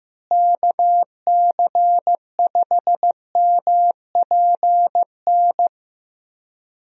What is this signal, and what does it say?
Morse code, heard as a keyed tone, about 15 wpm, 700 Hz. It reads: KC5MPN
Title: USA Call Signs - 6 Characters - Repeated 2x - 15wpm